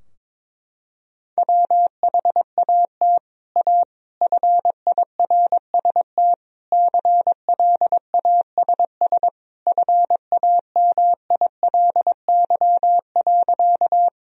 Morse code, heard as a keyed tone, about 22 wpm, 700 Hz. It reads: WHAT A FIRST CLASS FAMILY.
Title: Sentences from Top 1000 Words - Review Part 2 - 22wpm